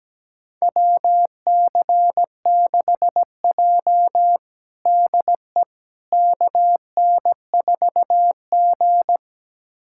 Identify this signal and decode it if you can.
Morse code, heard as a keyed tone, about 17 wpm, 700 Hz. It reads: WC6J DE KN4G
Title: USA Callsign DE Callsign - 4 Characters - 17wpm